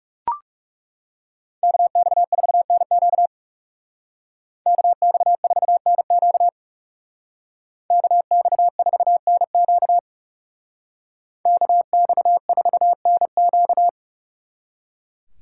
Morse code, heard as a keyed tone, about 39 wpm, 700 Hz. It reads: KX4NQ KX4NQ KX4NQ KX4NQ